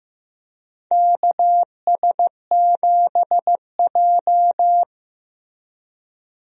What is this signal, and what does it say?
Morse code, heard as a keyed tone, about 15 wpm, 700 Hz. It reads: KS7J
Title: USA Call Signs - 4 Characters - Repeated 2x - 15wpm